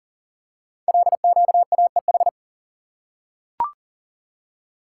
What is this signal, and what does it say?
Morse code, heard as a keyed tone, about 39 wpm, 700 Hz. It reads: LQAEH
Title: Code Groups - 5 Letters - 40wpm